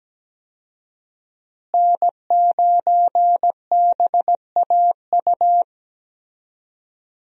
Morse code, heard as a keyed tone, about 17 wpm, 700 Hz. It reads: N9BAU
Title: USA Call Signs - 5 Characters - Repeated 3x - 17wpm